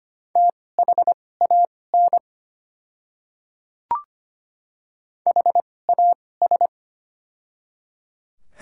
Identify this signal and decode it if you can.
Morse code, heard as a keyed tone, about 25 wpm, spaced as for 18 wpm, 700 Hz. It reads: THAN HAS
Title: 2x Character Spacing - Top 200 Words - 25wpm